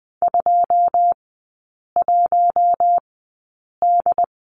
Morse code, heard as a keyed tone, about 20 wpm, spaced as for 8 wpm, 700 Hz. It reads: 21D
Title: 107 Lesson 18e - Letter m - Three Characters 20wpm